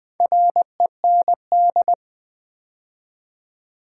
Morse code, heard as a keyed tone, about 20 wpm, 700 Hz. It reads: REND
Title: Build A Word - 2 Letters At A Time - 6-Letter Words - 20wpm